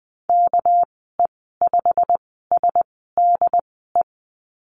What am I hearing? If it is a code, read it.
Morse code, heard as a keyed tone, about 20 wpm, spaced as for 15 wpm, 700 Hz. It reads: KE5SDE
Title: 200 Lesson 32g - Letter k - Call Signs 20wpm